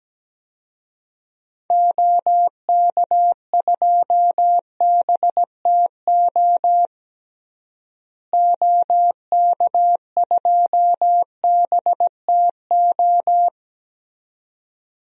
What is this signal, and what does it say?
Morse code, heard as a keyed tone, about 17 wpm, 700 Hz. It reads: OK2BTO OK2BTO
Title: DXCC-100 Call Signs - 6 Characters - Repeated 2x - 17wpm